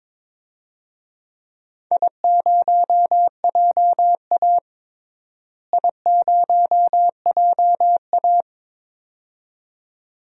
Morse code, heard as a keyed tone, about 22 wpm, 700 Hz. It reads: I0JA I0JA
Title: DXCC-100 Call Signs - 4 Characters - Repeated 2x - 22wpm